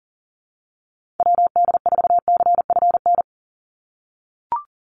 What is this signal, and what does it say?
Morse code, heard as a keyed tone, about 40 wpm, 700 Hz. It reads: WD4CFN
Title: NA SOTA Chaser Callsigns 40wpm